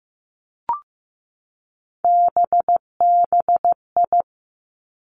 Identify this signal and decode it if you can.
Morse code, heard as a keyed tone, about 15 wpm, 700 Hz. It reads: BBI
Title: Three Challenging Letter-Number Combinations 15wpm